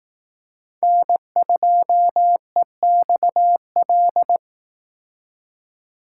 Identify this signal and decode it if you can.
Morse code, heard as a keyed tone, about 19 wpm, 700 Hz. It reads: N2EXL
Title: Speed Racing - USA 1x3 Call Signs 15wpm